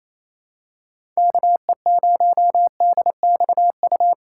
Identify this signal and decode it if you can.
Morse code, heard as a keyed tone, about 28 wpm, 700 Hz. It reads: KE0DXU
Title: USA Call Signs - 6 Characters - 8x Character Spacing - 28wpm